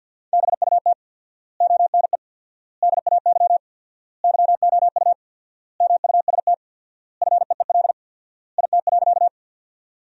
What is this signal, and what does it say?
Morse code, heard as a keyed tone, about 47 wpm, 700 Hz. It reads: BUT ONE DAY YOU MUST FEEL IT.
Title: 4x Word-Spacing - Sentences from Top 400 Words - 50wpm